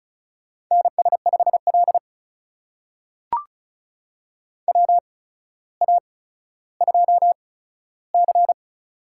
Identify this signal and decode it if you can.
Morse code, heard as a keyed tone, about 36 wpm, spaced as for 15 wpm, 700 Hz. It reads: NS5L WA2C
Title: USA Call Signs - 4 Characters - 8x Character Spacing - 35wpm